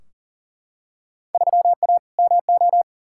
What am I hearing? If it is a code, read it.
Morse code, heard as a keyed tone, about 37 wpm, 700 Hz. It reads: 3A MO
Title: Field Day Exchanges Encore - 40wpm